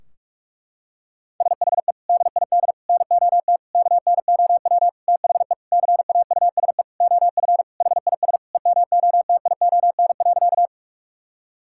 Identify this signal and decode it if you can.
Morse code, heard as a keyed tone, about 41 wpm, 700 Hz. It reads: SHE DID NOT KNOW THE CAUSE OF HIS EMOTION.
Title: Sentences from Top 1600 Words 45wpm